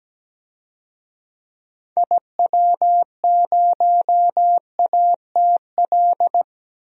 Morse code, heard as a keyed tone, about 17 wpm, 700 Hz. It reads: IW0ATL
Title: DXCC-100 Call Signs - 6 Characters - 17wpm